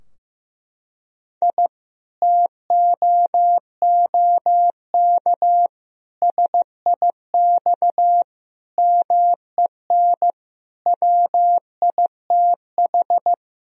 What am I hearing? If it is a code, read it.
Morse code, heard as a keyed tone, about 15 wpm, 700 Hz. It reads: I TOOK SIX MEN WITH
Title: Sentences from Top 1000 Words - Review Part 2 - 15wpm